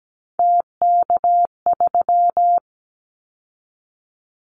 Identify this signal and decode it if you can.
Morse code, heard as a keyed tone, about 17 wpm, 700 Hz. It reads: TK3
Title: Callsign Prefixes for Top 100 DXCC - 17wpm